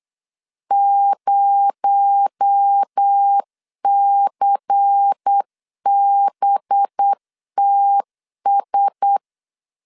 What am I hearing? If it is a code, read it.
Morse code, heard as a keyed tone, about 8 wpm, 800 Hz. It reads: 0CBTS
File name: LEZIONI / 8 YJ45 / 8 Y 8wpm 3245